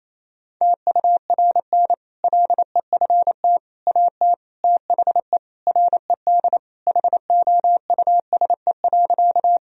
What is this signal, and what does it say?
Morse code, heard as a keyed tone, about 28 wpm, 700 Hz. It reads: TURN LEFT AT THE RED HOUSE.
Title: Sentences from Top 1000 Words - Review Part 2 - 28wpm